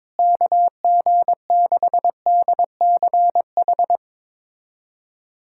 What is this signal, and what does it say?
Morse code, heard as a keyed tone, about 22 wpm, 700 Hz. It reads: KG6DCH
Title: USA Callsign DE Callsign 22wpm